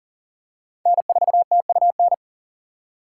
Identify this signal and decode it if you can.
Morse code, heard as a keyed tone, about 40 wpm, 700 Hz. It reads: N4TUN
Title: USA Call Signs - 5 Characters - 6x Character Spacing - 40wpm